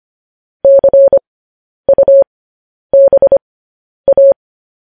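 Morse code, heard as a keyed tone, about 25 wpm, spaced as for 10 wpm, 550 Hz. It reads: CUBA
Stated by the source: Morse news headlines at 10.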